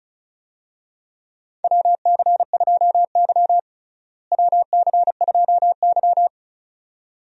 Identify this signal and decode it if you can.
Morse code, heard as a keyed tone, about 34 wpm, 700 Hz. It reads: WC2Y WC2Y
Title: USA Call Signs - 4 Characters - Repeated 2x - 35wpm